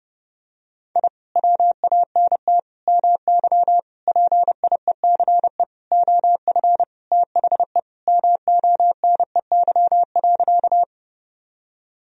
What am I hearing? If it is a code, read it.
Morse code, heard as a keyed tone, about 30 wpm, 700 Hz. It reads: I WANT MY PIECE OF THE MONEY.
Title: Sentences from Top 1000 Words - Review Part 2 - 30wpm